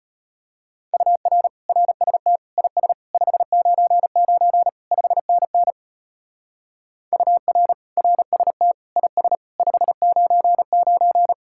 Morse code, heard as a keyed tone, about 30 wpm, 700 Hz. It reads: UR RST IS 599 5NN UR RST IS 599
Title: Speed-Racing Common QSO Phrases - 25wpm